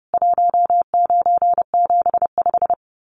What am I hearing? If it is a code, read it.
Morse code, heard as a keyed tone, about 30 wpm, 700 Hz. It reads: 1975
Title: Common QSO Phrases 30wpm